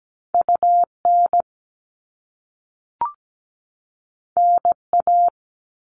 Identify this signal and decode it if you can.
Morse code, heard as a keyed tone, about 17 wpm, 700 Hz. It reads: UN NA
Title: All Cut Numbers - 2-Digit 17wpm